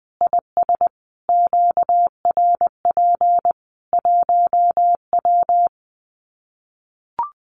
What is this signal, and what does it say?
Morse code, heard as a keyed tone, about 20 wpm, 700 Hz. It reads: IS QRP 1W
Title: Common QSO Phrases 20wpm